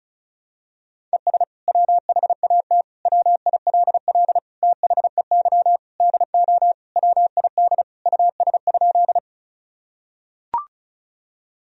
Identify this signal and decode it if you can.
Morse code, heard as a keyed tone, about 35 wpm, 700 Hz. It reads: ES WHAT WILL THEY DO WID US?
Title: Sentences from Top 200 Words and Abbreviations 35wpm